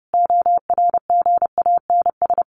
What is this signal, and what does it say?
Morse code, heard as a keyed tone, about 30 wpm, 700 Hz. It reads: ORGANS
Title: A Sound Of Thunder 30wpm